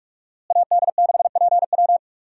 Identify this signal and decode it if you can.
Morse code, heard as a keyed tone, about 41 wpm, 700 Hz. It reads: AD6PW
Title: USA Call Signs - 5 Characters - Repeated 2x - 45wpm